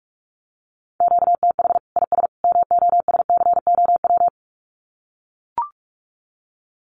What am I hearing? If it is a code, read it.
Morse code, heard as a keyed tone, about 44 wpm, 700 Hz. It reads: QTH IS MOSCOW
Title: Common QSO Phrases 45wpm